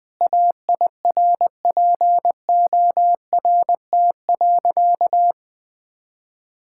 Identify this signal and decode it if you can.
Morse code, heard as a keyed tone, about 20 wpm, 700 Hz. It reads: AIRPORT.
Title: Sentences with 3-Word Phrases - 20wpm